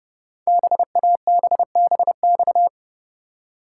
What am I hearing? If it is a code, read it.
Morse code, heard as a keyed tone, about 30 wpm, 700 Hz. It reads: BABBX
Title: Code Groups - 5 Letters - 30wpm